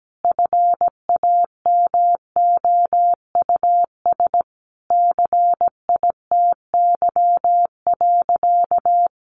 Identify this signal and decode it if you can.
Morse code, heard as a keyed tone, about 17 wpm, 700 Hz. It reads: FAMOUS CITY.
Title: Sentences from Top 1500 Words 17wpm